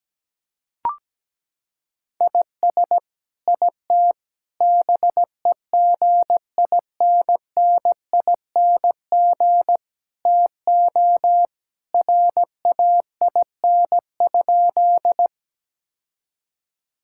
Morse code, heard as a keyed tone, about 17 wpm, 700 Hz. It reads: IS IT BEGINNING TO RAIN?
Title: Sentences from Top 1000 Words - Review Part 5 - 17wpm